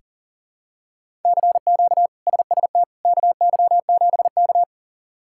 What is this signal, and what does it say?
Morse code, heard as a keyed tone, about 40 wpm, 700 Hz. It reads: CQ SST KY7K
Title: K1USN SST QSO Practice - 40wpm